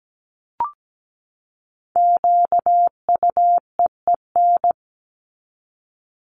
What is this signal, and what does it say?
Morse code, heard as a keyed tone, about 17 wpm, 700 Hz. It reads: QUEEN
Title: Top 1751-2000 Words 17wpm